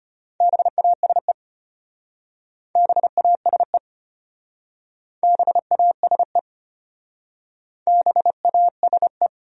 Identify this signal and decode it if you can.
Morse code, heard as a keyed tone, about 33 wpm, 700 Hz. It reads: BASE BASE BASE BASE